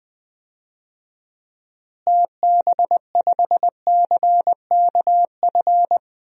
Morse code, heard as a keyed tone, about 20 wpm, 700 Hz. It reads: TB5CKF
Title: DXCC-100 Call Signs - 6 Characters - Repeated 2x - 20wpm